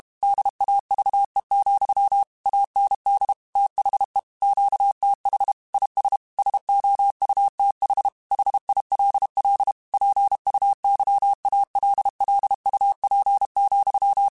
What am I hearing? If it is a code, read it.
Morse code, heard as a keyed tone, about 32 wpm, 800 Hz. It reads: DAVE, AND THE QTH IS SOUTH HILL PUYALLUP,
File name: QSOs / QSOs-mp3s / WA7QOX 2@30